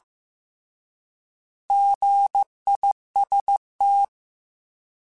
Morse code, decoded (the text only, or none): GIST